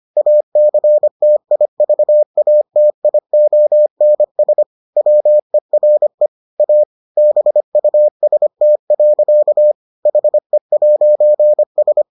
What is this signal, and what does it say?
Morse code, heard as a keyed tone, about 25 wpm, 600 Hz. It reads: ACTIVATIONS WERE A BUST. HE'S